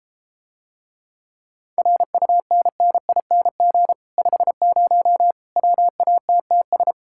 Morse code, heard as a keyed tone, about 33 wpm, 700 Hz. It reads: RUNNING 50 WATTS